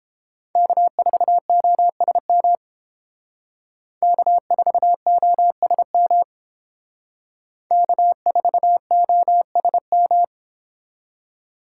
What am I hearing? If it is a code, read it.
Morse code, heard as a keyed tone, about 30 wpm, 700 Hz. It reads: K4OSM K4OSM K4OSM